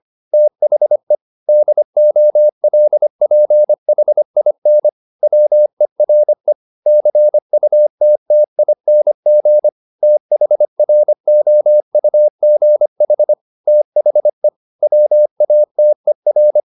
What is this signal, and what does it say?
Morse code, heard as a keyed tone, about 25 wpm, 600 Hz. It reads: THE DOLPHIN WERE CUTTING THROUGH THE WATER
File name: the old man and the sea part 1 25wpm